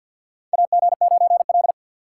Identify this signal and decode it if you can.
Morse code, heard as a keyed tone, about 50 wpm, 700 Hz. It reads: AG9L
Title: USA Call Signs - 4 Characters - 3x Character Spacing - 50wpm